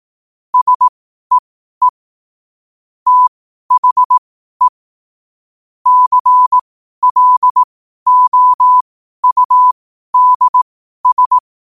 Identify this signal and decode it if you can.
Morse code, heard as a keyed tone, about 18 wpm, spaced as for 12 wpm, 1000 Hz. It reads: SEE THE CLOUDS
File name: qotdfiles / 12wpm